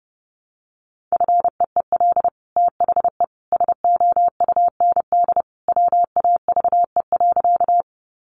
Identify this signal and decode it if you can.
Morse code, heard as a keyed tone, about 30 wpm, 700 Hz. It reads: FEEL THE SOUND WAVE.
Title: Sentences from Top 1400 Words 30wpm